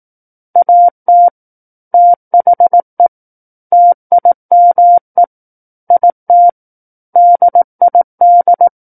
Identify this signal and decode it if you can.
Morse code, heard as a keyed tone, about 18 wpm, 700 Hz. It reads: AT THE TIME IT DID
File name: 18 WPM